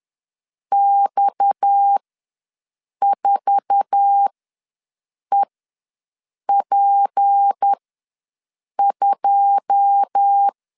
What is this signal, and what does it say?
Morse code, heard as a keyed tone, about 10 wpm, spaced as for 6 wpm, 800 Hz. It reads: X4EP2
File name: LEZIONI / 9 2 ws 10wpm 4130